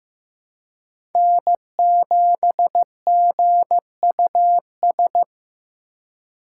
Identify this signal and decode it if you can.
Morse code, heard as a keyed tone, about 15 wpm, 700 Hz. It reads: N7GUS
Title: USA Call Signs - 2x Character Spacing - 15wpm